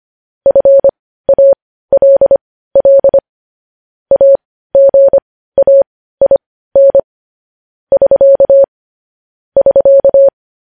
Morse code, heard as a keyed tone, about 25 wpm, spaced as for 15 wpm, 550 Hz. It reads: FALL AGAIN <SK> <SK>
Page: news-headlines-in-morse-code-at-15-wpm